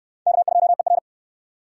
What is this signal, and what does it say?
Morse code, heard as a keyed tone, about 63 wpm, 700 Hz. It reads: K2ER